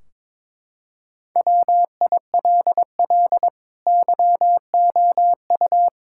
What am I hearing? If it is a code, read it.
Morse code, heard as a keyed tone, about 22 wpm, 700 Hz. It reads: WILL YOU
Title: Sentences from Top 1600 Words 22wpm